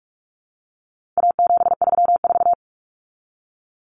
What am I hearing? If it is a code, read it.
Morse code, heard as a keyed tone, about 44 wpm, 700 Hz. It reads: A734
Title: Common Cut Numbers - 4-Digit 45wpm